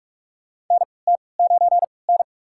Morse code, heard as a keyed tone, about 43 wpm, spaced as for 25 wpm, 700 Hz. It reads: NT9N
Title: USA Call Signs - 3x Character Spacing - 45wpm